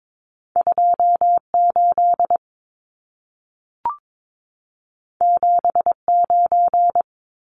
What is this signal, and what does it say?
Morse code, heard as a keyed tone, about 22 wpm, 700 Hz. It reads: 28 79
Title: Numbers - 2-Digit 22wpm